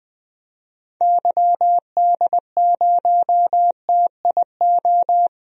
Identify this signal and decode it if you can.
Morse code, heard as a keyed tone, about 20 wpm, 700 Hz. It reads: YD0TIO
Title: DXCC-100 Call Signs - 6 Characters - 20wpm